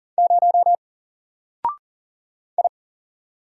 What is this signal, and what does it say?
Morse code, heard as a keyed tone, about 40 wpm, 700 Hz. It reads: 0 I
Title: Single Character (Letter:Number:Punctuation:Prosign) Alt 40wpm